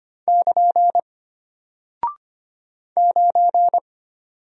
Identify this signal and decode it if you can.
Morse code, heard as a keyed tone, about 25 wpm, 700 Hz. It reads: <KN> 9
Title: Single Character (Letter:Number:Punctuation:Prosign) Alt 25wpm